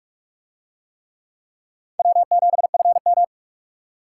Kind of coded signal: Morse code

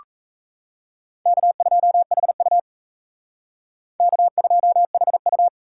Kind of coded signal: Morse code

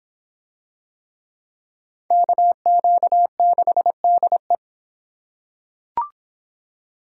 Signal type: Morse code

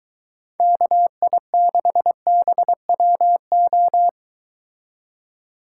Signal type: Morse code

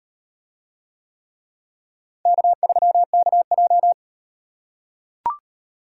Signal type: Morse code

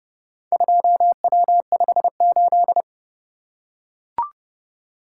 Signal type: Morse code

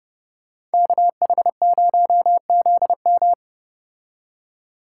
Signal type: Morse code

